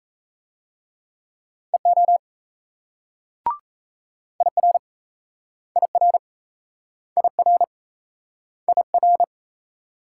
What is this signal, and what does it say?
Morse code, heard as a keyed tone, about 38 wpm, 700 Hz. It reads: EO IR IR IR IR